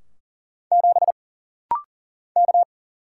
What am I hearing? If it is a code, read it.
Morse code, heard as a keyed tone, about 40 wpm, 700 Hz. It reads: 7 K
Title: Single Letter-Number Alt - Rapid-Fire 40wpm